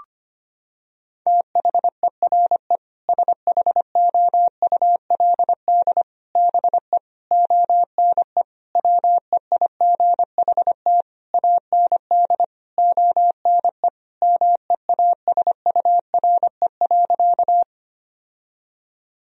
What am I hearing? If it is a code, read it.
Morse code, heard as a keyed tone, about 25 wpm, 700 Hz. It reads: THERE SHOULD BE ONE WEIGHT AND ONE MEASURE.
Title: Sentences from Top 1000 Words - Review Part 3 - 25wpm